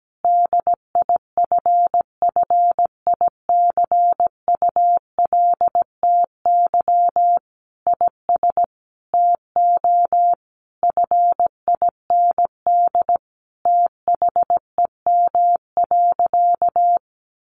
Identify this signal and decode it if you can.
Morse code, heard as a keyed tone, about 17 wpm, 700 Hz. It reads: DIFFICULTY IS TO FIND THEM.
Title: Sentences from Top 1500 Words 17wpm